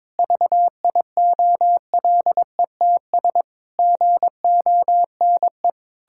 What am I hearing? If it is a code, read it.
Morse code, heard as a keyed tone, about 22 wpm, 700 Hz. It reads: VIOLETS GONE
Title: Pangram Challenge v2 - 22wpm